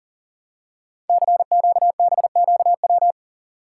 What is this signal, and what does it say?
Morse code, heard as a keyed tone, about 38 wpm, 700 Hz. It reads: CQBQW
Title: Code Groups - 5 Letter:Numbers - 40wpm